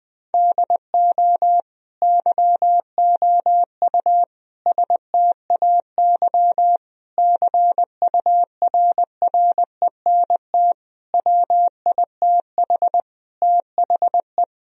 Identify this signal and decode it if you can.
Morse code, heard as a keyed tone, about 20 wpm, 700 Hz. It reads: DO YOU STAY CURRENT WITH THE
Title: Sentences from Top 1000 Words - Review Part 4 - 20wpm